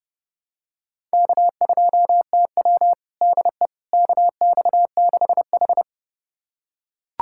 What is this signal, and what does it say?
Morse code, heard as a keyed tone, about 31 wpm, 700 Hz. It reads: K2TW DE KX6H
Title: USA Callsign DE Callsign - 4 Characters - 30wpm